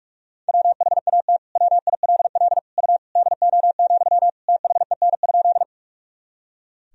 Morse code, heard as a keyed tone, about 46 wpm, 700 Hz. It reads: WHAT WILL U DO, THEN?
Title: Sentences from Top 100 Words and Abbreviations 45wpm